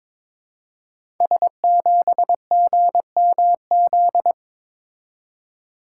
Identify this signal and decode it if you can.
Morse code, heard as a keyed tone, about 23 wpm, 700 Hz. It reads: S7GMZ
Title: Code Groups - 5 Letter:Numbers - 22wpm